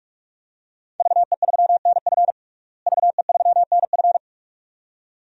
Morse code, heard as a keyed tone, about 46 wpm, 700 Hz. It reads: VE3NF VE3NF
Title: DXCC-100 Call Signs - 5 Characters - Repeated 2x - 45wpm